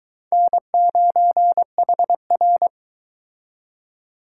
Morse code, heard as a keyed tone, about 22 wpm, 700 Hz. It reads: N9HR